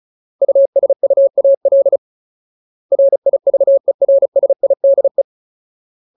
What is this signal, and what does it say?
Morse code, heard as a keyed tone, about 34 wpm, 550 Hz. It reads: USUAL RIVERSIDE
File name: Ultimate Word List / 35WPM Ultimate